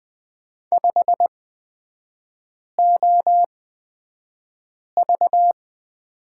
Morse code, heard as a keyed tone, about 20 wpm, spaced as for 5 wpm, 700 Hz. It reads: 5OV
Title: 255 Lesson 41e - Period - Three Characters 20wpm